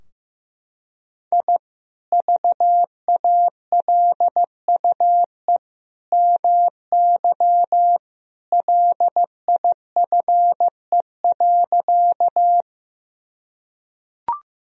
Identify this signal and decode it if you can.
Morse code, heard as a keyed tone, about 15 wpm, 700 Hz. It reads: I VALUE MY LIFE.
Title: Sentences from Top 1000 Words - Review Part 3 - 15wpm